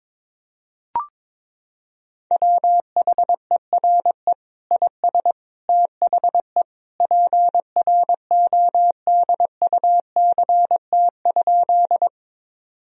Morse code, heard as a keyed tone, about 22 wpm, 700 Hz. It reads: WHERE IS THE PRODUCT?
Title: Sentences from Top 1000 Words - Review Part 2 - 22wpm